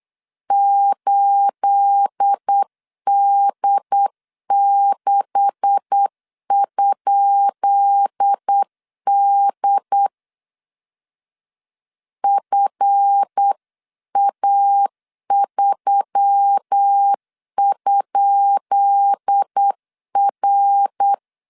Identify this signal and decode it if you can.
Morse code, heard as a keyed tone, about 8 wpm, 800 Hz. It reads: 8D6?D FA3?R
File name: LEZIONI / 10 SIGNS / 10B INT 8wpm 3534